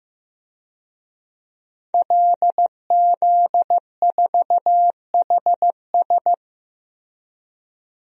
Morse code, heard as a keyed tone, about 15 wpm, 700 Hz. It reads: LZ4HS
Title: DXCC-100 Call Signs - 5 Characters - Repeated 3x - 15wpm